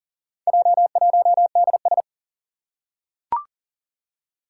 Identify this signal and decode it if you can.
Morse code, heard as a keyed tone, about 40 wpm, 700 Hz. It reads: J1DS
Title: Four Challenging Letter-Number Combinations 40wpm